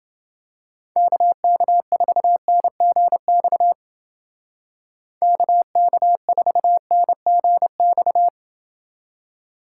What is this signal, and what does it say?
Morse code, heard as a keyed tone, about 29 wpm, 700 Hz. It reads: KK4NGX KK4NGX